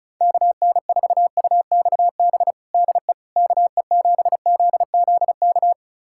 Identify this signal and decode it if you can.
Morse code, heard as a keyed tone, about 34 wpm, 700 Hz. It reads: KN4UXB DE KE7ZZK
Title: USA Callsign DE Callsign - 6 Characters - 35wpm